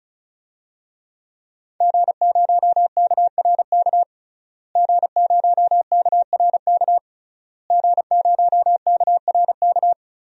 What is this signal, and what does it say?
Morse code, heard as a keyed tone, about 34 wpm, 700 Hz. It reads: G0KRK G0KRK G0KRK
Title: DXCC-100 Call Signs - 5 Characters - Repeated 3x - 35wpm